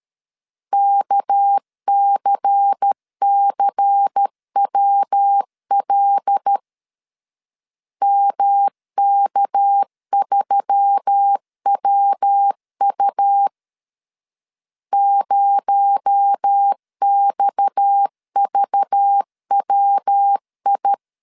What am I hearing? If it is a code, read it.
Morse code, heard as a keyed tone, about 13 wpm, 800 Hz. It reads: KCCWL MK3WU 0XVWI
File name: CW / DET / 1 6 1500C S A 12wpm 3440